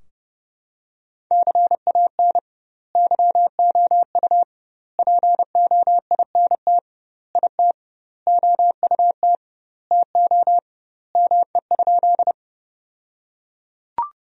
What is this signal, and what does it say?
Morse code, heard as a keyed tone, about 32 wpm, 700 Hz. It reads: CAN YOU POINT IT OUT TO ME?
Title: 2x Word-Spacing - Sentences from Top 200 Words - 30wpm